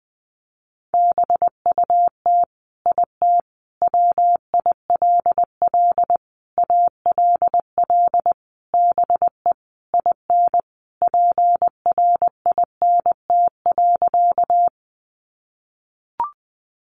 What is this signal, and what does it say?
Morse code, heard as a keyed tone, about 20 wpm, 700 Hz. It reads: BUT IT WILL ALL BE IN PRINT.
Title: Sentences from Top 1400 Words 20wpm